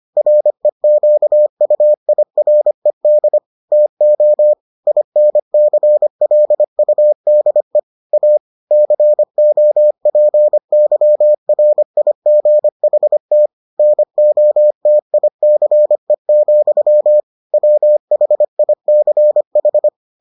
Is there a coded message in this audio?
Yes: Morse code from 0.2 to 19.9 s, heard as a keyed tone, about 25 wpm, 600 Hz. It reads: REQUIRED TO INCLUDE A COPYRIGHT NOTICE, WHICH